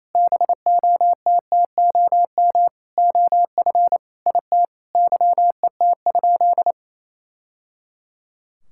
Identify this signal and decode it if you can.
Morse code, heard as a keyed tone, about 28 wpm, 700 Hz. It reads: BOTTOM OF IT YET?
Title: Sentences from Top 1000 Words - Review Part 6 - 28wpm